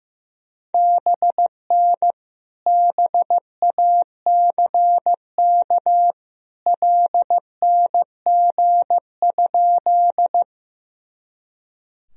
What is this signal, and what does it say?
Morse code, heard as a keyed tone, about 15 wpm, 700 Hz. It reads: BN BACK LNG?
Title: Sentences from Top 200 Words and Abbreviations 15wpm